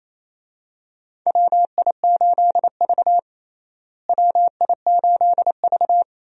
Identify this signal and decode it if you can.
Morse code, heard as a keyed tone, about 28 wpm, 700 Hz. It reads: WI8V WI8V
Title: USA Call Signs - 4 Characters - Repeated 3x - 28wpm